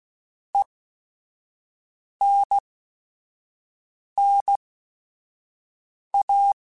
Morse code, heard as a keyed tone, about 16 wpm, spaced as for 5 wpm, 800 Hz. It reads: ENNA